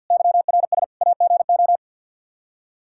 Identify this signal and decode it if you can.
Morse code, heard as a keyed tone, about 53 wpm, 700 Hz. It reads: YRS AGO
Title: Practice QSO - Exchange 3 - More Info - 50wpm